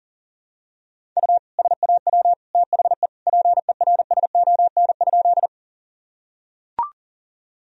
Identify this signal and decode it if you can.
Morse code, heard as a keyed tone, about 40 wpm, 700 Hz. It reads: U SAW THE PERSON?
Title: Sentences from Top 400 Words and Abbreviations 40wpm